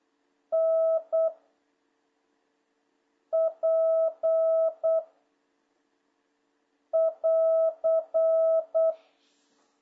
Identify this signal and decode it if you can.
Morse code, heard as a keyed tone, about 8 wpm, 1300 Hz, with noise. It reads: N P <AR>
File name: jh8cbh / 2026cw